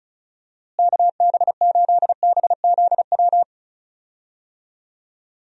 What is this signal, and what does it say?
Morse code, heard as a keyed tone, about 36 wpm, 700 Hz. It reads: KB8BZW
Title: USA Call Signs - 6 Characters - Repeated 3x - 35wpm